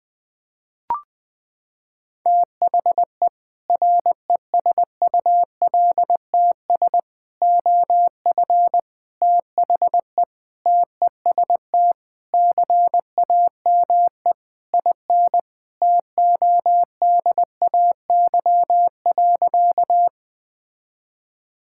Morse code, heard as a keyed tone, about 20 wpm, 700 Hz. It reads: THE RESULTS OF THE TEST CAME IN TODAY.
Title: Sentences with 4-Word Phrases - 20wpm